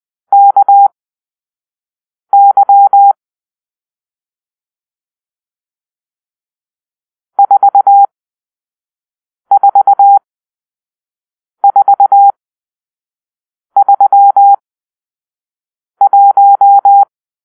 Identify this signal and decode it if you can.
Morse code, heard as a keyed tone, about 20 wpm, spaced as for 5 wpm, 800 Hz. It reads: KY 44431
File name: arrl / Grp-6 Drill-4